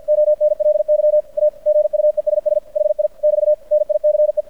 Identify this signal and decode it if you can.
Morse code, heard as a keyed tone, about 55 wpm, 600 Hz, with radio noise. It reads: ONLY A GUESS HI <BT> NICE
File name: archived / CRSnet